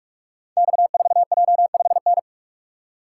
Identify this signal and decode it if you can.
Morse code, heard as a keyed tone, about 44 wpm, 690 Hz. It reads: X4J5N
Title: Code Groups - 5 Letter:Numbers - 45wpm